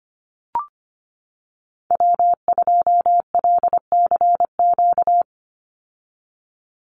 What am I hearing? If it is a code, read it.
Morse code, heard as a keyed tone, about 25 wpm, 700 Hz. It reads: W2LCQ
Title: Call Signs - 25wpm